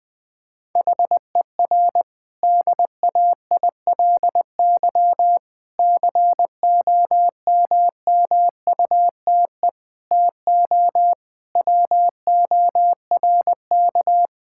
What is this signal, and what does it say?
Morse code, heard as a keyed tone, about 20 wpm, 700 Hz. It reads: HER DAILY COMMUTE TO WORK
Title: Difficult Sentences with Hints - 20wpm